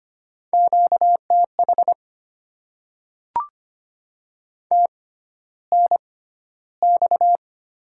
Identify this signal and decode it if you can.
Morse code, heard as a keyed tone, about 25 wpm, spaced as for 13 wpm, 700 Hz. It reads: QTH TNX
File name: Top 20 QSO Elements - 6x Character Spacing - 25wpm